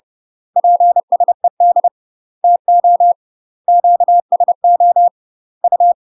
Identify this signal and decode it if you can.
Morse code, heard as a keyed tone, about 30 wpm, 700 Hz. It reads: PSED TO QSO U
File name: UK MASTER QSOs TRAINING / 30WPM / UK MASTER QSOs 0030